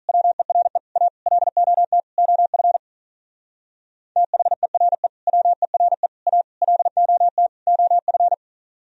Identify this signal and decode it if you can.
Morse code, heard as a keyed tone, about 42 wpm, 700 Hz. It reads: WERE A LOT OF THERE WERE A LOT OF
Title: Speed-Racing - Sets of 5 Words 35wpm